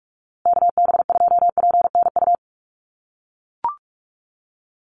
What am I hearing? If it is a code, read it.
Morse code, heard as a keyed tone, about 44 wpm, 700 Hz. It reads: KB2PNU
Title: USA Call Signs Repeated 2x 45wpm